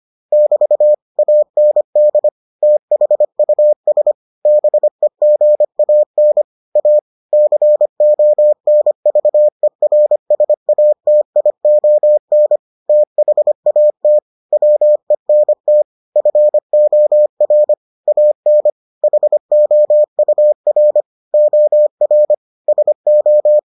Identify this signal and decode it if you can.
Morse code, heard as a keyed tone, about 25 wpm, 600 Hz. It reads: <BT> AND THUS BEGAN A CONVERSATION THAT WENT FOR AN HOUR OR SO